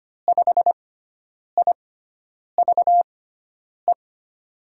Morse code, heard as a keyed tone, about 25 wpm, spaced as for 9 wpm, 700 Hz. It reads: 5IVE
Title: USA Call Signs - 6x Character Spacing - 25wpm